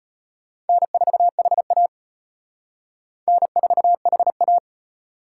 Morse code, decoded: N4HA N4HA

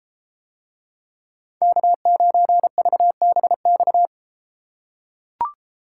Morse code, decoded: K9VBX

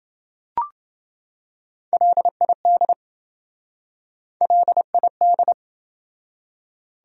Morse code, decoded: LID LID

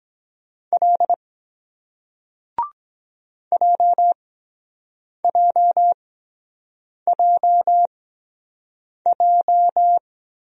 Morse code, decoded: L JJJJ

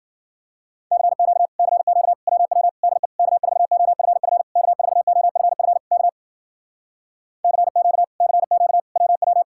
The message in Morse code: CQ CQ WW DE K4OFF K4OFF K CQ CQ WW

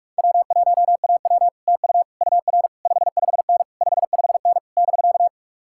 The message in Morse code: W1AW TU UR 55N 55N <BK>

